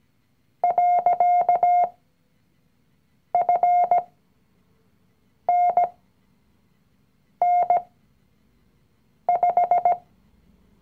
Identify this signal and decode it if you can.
Morse code, heard as a keyed tone, about 17 wpm, spaced as for 5 wpm, 710 Hz. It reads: .FNN5
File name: CW-Online-Kurs / Anfänger / Lektion 019